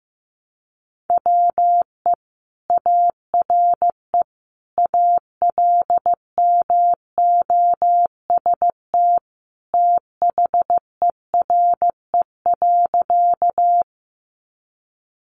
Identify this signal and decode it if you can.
Morse code, heard as a keyed tone, about 15 wpm, 700 Hz. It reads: WE ARE ALMOST THERE.
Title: Sentences from Top 700 Words 15wpm